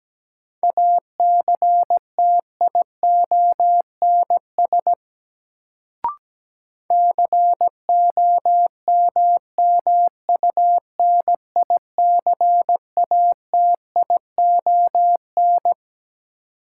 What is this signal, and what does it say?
Morse code, heard as a keyed tone, about 17 wpm, 700 Hz. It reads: ACTIONS COMMUNICATION
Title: Sentences from Top 1000 Words - Review Part 6 - New Word Drill - 17wpm